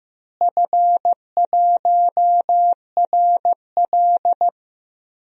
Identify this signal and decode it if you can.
Morse code, heard as a keyed tone, about 15 wpm, 700 Hz. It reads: F1RL
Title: DXCC-100 Call Signs - 4 Characters - Repeated 2x - 15wpm